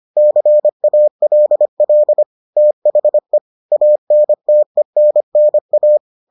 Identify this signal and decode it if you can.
Morse code, heard as a keyed tone, about 25 wpm, 600 Hz. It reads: CALL THE ANTENNA